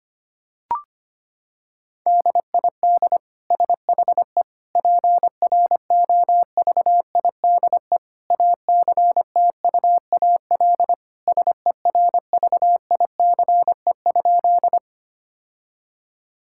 Morse code, heard as a keyed tone, about 25 wpm, 700 Hz. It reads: DID SHE PROVIDE ACTUAL SERVICE?
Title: Sentences from Top 1000 Words - Review Part 5 - 25wpm